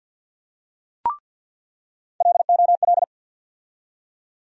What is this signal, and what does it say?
Morse code, heard as a keyed tone, about 47 wpm, 690 Hz. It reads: LOL
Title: USA Call Sign Postfixes - 3 Letters 50wpm